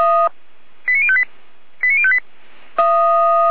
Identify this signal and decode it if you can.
Five-tone selective call (ZVEI), heard as quick runs of five tones, tones of about 70 ms. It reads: 89058 89058